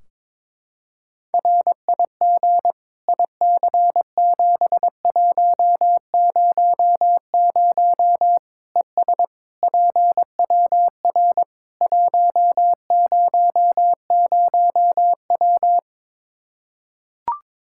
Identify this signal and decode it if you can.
Morse code, heard as a keyed tone, about 22 wpm, 700 Hz. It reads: RIG IC7100 ES PWR 100W
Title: Practice QSO - Exchange 2 - Info - 22wpm